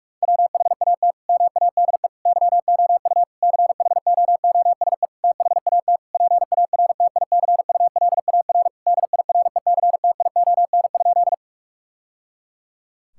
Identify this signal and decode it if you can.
Morse code, heard as a keyed tone, about 46 wpm, 700 Hz. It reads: WHAT MADE YOU CHOOSE THAT PARTICULAR DIRECTION?
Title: Sentences from Top 1000 Words - Review Part 5 - 45wpm